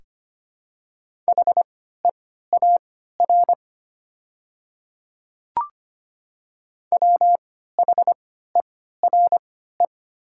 Morse code, heard as a keyed tone, about 25 wpm, spaced as for 14 wpm, 700 Hz. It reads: HEAR WHERE
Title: 3x Character Spacing - Top 300 Words - 25wpm